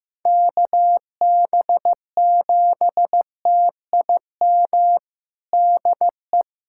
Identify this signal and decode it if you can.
Morse code, heard as a keyed tone, about 15 wpm, 700 Hz. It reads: KB7TIM DE
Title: USA Callsign DE Callsign - 6 Characters - 15wpm